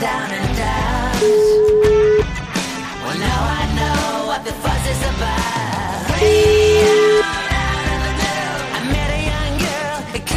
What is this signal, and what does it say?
Ringback tone (Europe, 425 Hz), with noise.